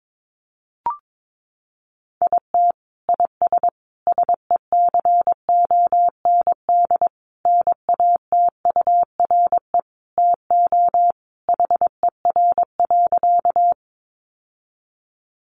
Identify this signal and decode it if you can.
Morse code, heard as a keyed tone, about 22 wpm, 700 Hz. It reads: IT IS SECOND NATURE TO HER.
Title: Sentences from Top 600 Words 22wpm